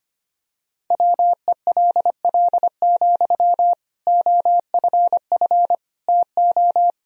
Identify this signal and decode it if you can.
Morse code, heard as a keyed tone, about 25 wpm, 700 Hz. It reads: WELL, OFF TO
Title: Sentences from Top 1000 Words - Review Part 3 - 25wpm